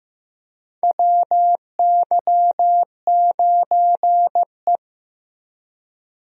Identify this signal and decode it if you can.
Morse code, heard as a keyed tone, about 15 wpm, 700 Hz. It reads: WY9E
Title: USA Call Signs - 4 Characters - Repeated 2x - 15wpm